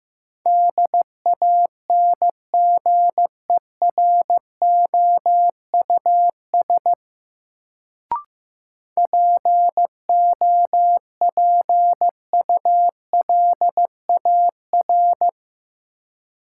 Morse code, heard as a keyed tone, about 15 wpm, 700 Hz. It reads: DANGEROUS POPULAR
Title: Sentences from Top 1200 Words - New Word Drill Alt - 15wpm